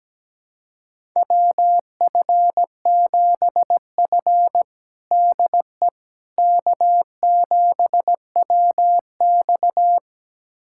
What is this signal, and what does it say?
Morse code, heard as a keyed tone, about 17 wpm, 700 Hz. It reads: WF7F DE K7WX
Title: USA Callsign DE Callsign - 4 Characters - 17wpm